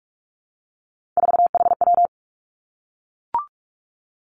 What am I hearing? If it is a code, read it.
Morse code, heard as a keyed tone, about 46 wpm, 700 Hz. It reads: 4HW
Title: Three Challenging Letter-Number Combinations 45wpm